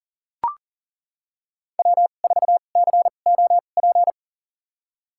Morse code, heard as a keyed tone, about 40 wpm, spaced as for 29 wpm, 700 Hz. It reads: W4COP
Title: USA Call Signs - 5 Characters - 2x Character Spacing - 40wpm